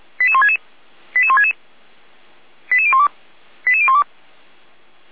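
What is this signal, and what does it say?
Five-tone selective call (ZVEI), heard as quick runs of five tones, tones of about 70 ms. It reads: 88166 88166 88012 88012
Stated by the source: ZVEI five-tone selective call received on an SDR, from the Feuerwehr Orlamünde (fire brigade) audio server.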